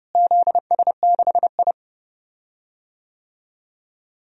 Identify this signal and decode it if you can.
Morse code, heard as a keyed tone, about 30 wpm, 700 Hz. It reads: ZS6I
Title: DXCC-100 Call Signs - 4 Characters - Repeated 2x - 30wpm